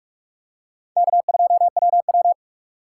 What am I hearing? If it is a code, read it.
Morse code, heard as a keyed tone, about 46 wpm, 700 Hz. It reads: K2WW
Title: USA Call Signs - 4 Characters - 2x Character Spacing - 45wpm